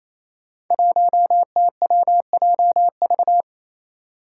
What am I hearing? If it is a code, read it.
Morse code, heard as a keyed tone, about 28 wpm, 700 Hz. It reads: 1TWJV
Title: Five Challenging Letter-Number Combinations 28wpm